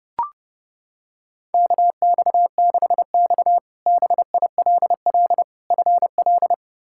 Morse code, heard as a keyed tone, about 30 wpm, 700 Hz. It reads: KX6X BILL FL
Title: K1USN SST QSO Practice - 30wpm